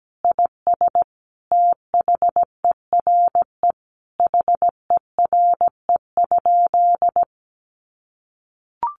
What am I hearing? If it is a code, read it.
Morse code, heard as a keyed tone, about 17 wpm, 700 Hz. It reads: IS THERE HERE?
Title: Sentences from Top 1500 Words 17wpm